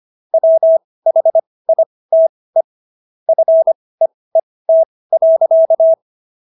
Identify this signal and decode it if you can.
Morse code, heard as a keyed tone, about 25 wpm, spaced as for 18 wpm, 650 Hz. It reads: WHITE FEET.